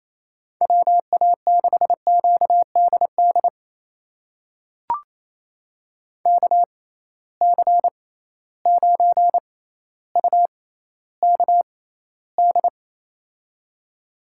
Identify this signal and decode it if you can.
Morse code, heard as a keyed tone, about 28 wpm, spaced as for 14 wpm, 700 Hz. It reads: WA6QDD KC9UKD
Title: USA Call Signs - 6 Characters - 6x Character Spacing - 28wpm